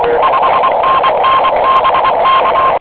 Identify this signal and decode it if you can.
Morse code, heard as a keyed tone, about 23 wpm, 1060 Hz, with radio noise. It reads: 5NN BK
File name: j7 vp2m 2002 / Voice / j75et 15 cw